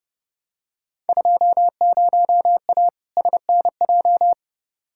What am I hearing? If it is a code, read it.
Morse code, heard as a keyed tone, about 30 wpm, 700 Hz. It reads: 20A SNJ
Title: Field Day Exchanges Encore - 30wpm